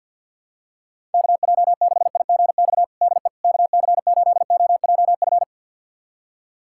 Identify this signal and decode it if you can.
Morse code, heard as a keyed tone, about 52 wpm, 700 Hz. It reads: KJ6IGX DE KC8OJF
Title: USA Callsign DE Callsign - 6 Characters - 50wpm